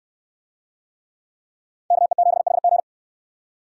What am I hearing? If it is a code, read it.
Morse code, heard as a keyed tone, about 69 wpm, 700 Hz. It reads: KE7HG